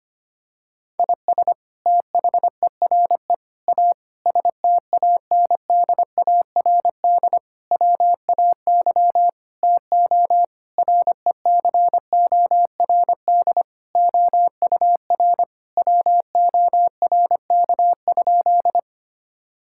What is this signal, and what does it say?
Morse code, heard as a keyed tone, about 25 wpm, 700 Hz. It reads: IS THERE A STANDARD WAY TO RECORD OUR WORK?
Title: Sentences from Top 1000 Words - Review Part 6 - 25wpm